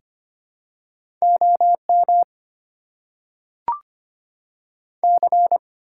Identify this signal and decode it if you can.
Morse code, heard as a keyed tone, about 25 wpm, 690 Hz. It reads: OM C
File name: Top 90 QSO Elements - 4x Character Spacing - 25wpm